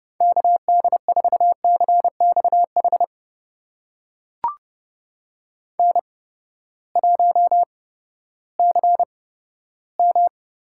Morse code, heard as a keyed tone, about 30 wpm, spaced as for 15 wpm, 700 Hz. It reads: KD4CXH N1CM
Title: USA Call Signs - 8x Character Spacing - 30wpm